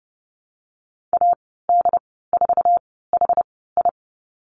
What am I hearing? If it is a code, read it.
Morse code, heard as a keyed tone, about 30 wpm, spaced as for 17 wpm, 700 Hz. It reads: AD4HI
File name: Lesson 14g - Number 2 - Call Signs 30wpm